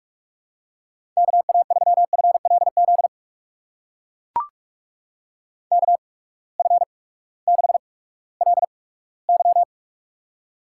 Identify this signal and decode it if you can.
Morse code, heard as a keyed tone, about 44 wpm, spaced as for 20 wpm, 700 Hz. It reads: KA3FLZ KF6LY
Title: USA Call Signs - 8x Character Spacing - 45wpm